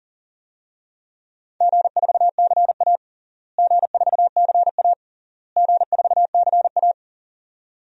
Morse code, heard as a keyed tone, about 40 wpm, 700 Hz. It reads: G4CA G4CA G4CA
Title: DXCC-100 Call Signs - 4 Characters - Repeated 3x - 40wpm